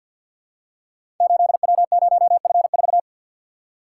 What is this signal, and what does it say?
Morse code, heard as a keyed tone, about 46 wpm, 700 Hz. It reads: 8W0F4
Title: Code Groups - 5 Letter:Numbers - 50wpm